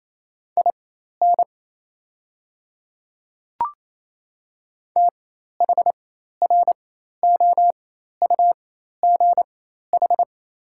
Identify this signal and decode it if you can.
Morse code, heard as a keyed tone, about 28 wpm, spaced as for 13 wpm, 700 Hz. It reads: IN THROUGH
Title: 4x Character Spacing - Top 200 Words - 28wpm